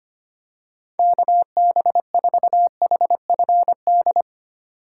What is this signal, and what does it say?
Morse code, heard as a keyed tone, about 25 wpm, 700 Hz. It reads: KB4HFD
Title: USA Call Signs - 2x Character Spacing - 25wpm